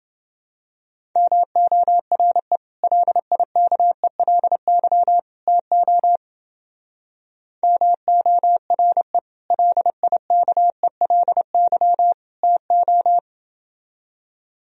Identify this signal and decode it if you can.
Morse code, heard as a keyed tone, about 28 wpm, 700 Hz. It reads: MORE LIKELY TO MORE LIKELY TO